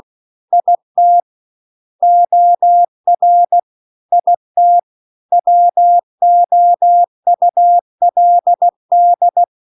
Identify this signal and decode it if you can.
Morse code, heard as a keyed tone, about 16 wpm, 700 Hz. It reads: IT OR IT WOULD
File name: CW-The-Big-Twoheaded-River-2-16WPM